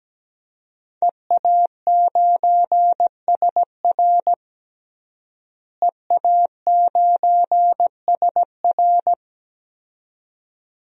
Morse code, heard as a keyed tone, about 17 wpm, 700 Hz. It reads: EA9SR EA9SR
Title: DXCC-100 Call Signs - 5 Characters - Repeated 2x - 17wpm